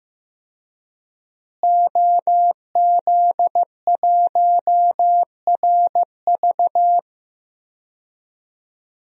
Morse code, heard as a keyed tone, about 15 wpm, 700 Hz. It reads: OZ1RV
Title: DXCC-100 Call Signs - 5 Characters - Repeated 2x - 15wpm